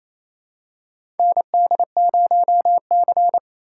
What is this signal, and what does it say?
Morse code, heard as a keyed tone, about 27 wpm, 700 Hz. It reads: ND0C
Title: USA Call Signs - 4 Characters - 8x Character Spacing - 28wpm